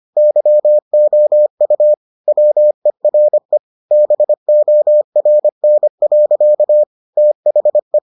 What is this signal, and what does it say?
Morse code, heard as a keyed tone, about 25 wpm, 600 Hz. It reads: YOU WERE BORN. THE